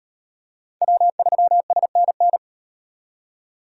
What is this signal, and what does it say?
Morse code, heard as a keyed tone, about 38 wpm, 700 Hz. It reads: W3SNN